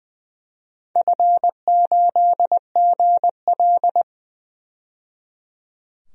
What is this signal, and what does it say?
Morse code, heard as a keyed tone, about 20 wpm, 700 Hz. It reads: F8GL
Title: DXCC-100 Call Signs - 4 Characters - 20wpm